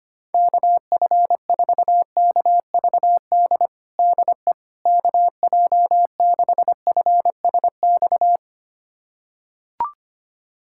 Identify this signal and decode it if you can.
Morse code, heard as a keyed tone, about 25 wpm, 700 Hz. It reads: KF4KVD DE KJ6FSX
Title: USA Callsign DE Callsign - 6 Characters - 25wpm